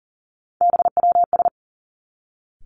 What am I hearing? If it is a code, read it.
Morse code, heard as a keyed tone, about 40 wpm, 700 Hz. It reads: BWS
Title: Three Challenging Letter-Number Combinations 40wpm